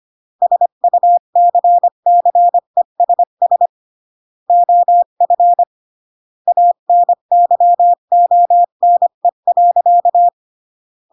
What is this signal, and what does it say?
Morse code, heard as a keyed tone, about 25 wpm, 700 Hz. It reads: SUCCESS OF ANYONE.
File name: CW / jodorowsky / 82 commandments 25-20